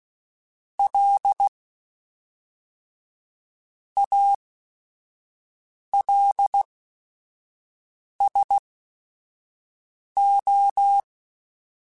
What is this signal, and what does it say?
Morse code, heard as a keyed tone, about 16 wpm, spaced as for 5 wpm, 800 Hz. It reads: L ALSO
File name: Technical Subjects / Technical Subjects-mp3s / XL @5